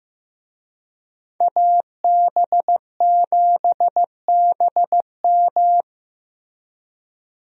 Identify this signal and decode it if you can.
Morse code, heard as a keyed tone, about 15 wpm, 700 Hz. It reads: AB7BM
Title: USA Call Signs - 5 Characters - Repeated 3x - 15wpm